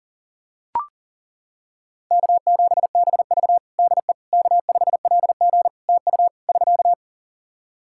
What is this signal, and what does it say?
Morse code, heard as a keyed tone, about 40 wpm, 700 Hz. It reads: K7BV DE K5LG TU <SK>
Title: K1USN SST QSO Practice 40wpm